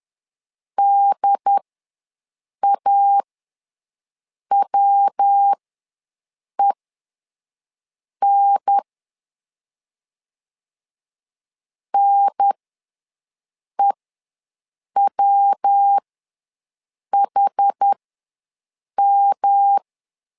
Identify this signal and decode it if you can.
Morse code, heard as a keyed tone, about 10 wpm, spaced as for 5 wpm, 800 Hz. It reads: DAWEN NEWHM
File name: LEZIONI / 3 MWRO / 3 MW ws 10wpm 2213